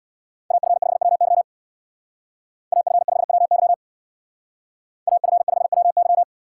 Morse code, heard as a keyed tone, about 62 wpm, 690 Hz. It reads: AF5WY AF5WY AF5WY
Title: Speed Racing - USA 2x2 Call Signs 50wpm